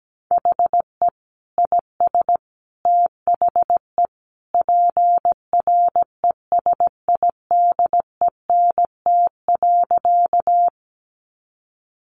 Morse code, heard as a keyed tone, about 17 wpm, 700 Hz. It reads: HE IS THE PRESIDENT.
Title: Sentences from Top 1000 Words 17wpm